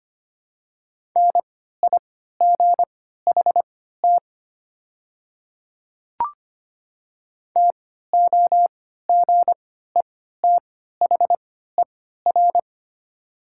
Morse code, heard as a keyed tone, about 25 wpm, spaced as for 14 wpm, 700 Hz. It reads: NIGHT TOGETHER
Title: 3x Character Spacing - Top 300 Words - 25wpm